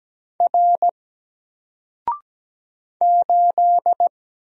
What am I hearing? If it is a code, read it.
Morse code, heard as a keyed tone, about 17 wpm, 700 Hz. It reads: R 8
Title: Single Letter-Number Alt 17wpm